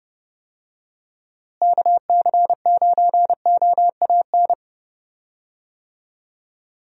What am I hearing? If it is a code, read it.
Morse code, heard as a keyed tone, about 30 wpm, 700 Hz. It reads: KC9OAN
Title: USA Call Signs - 6 Characters - Repeated 3x - 30wpm